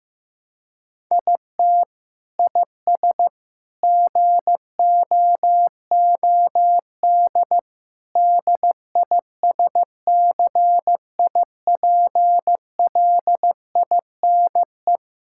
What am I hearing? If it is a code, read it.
Morse code, heard as a keyed tone, about 15 wpm, 700 Hz. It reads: IT IS GOOD DISCIPLINE
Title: Sentences from Top 1000 Words - Review Part 6 - 15wpm